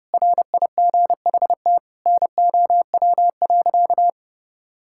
Morse code, heard as a keyed tone, about 30 wpm, 700 Hz. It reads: RIGHT NOW.
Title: Sentences from Top 1600 Words 30wpm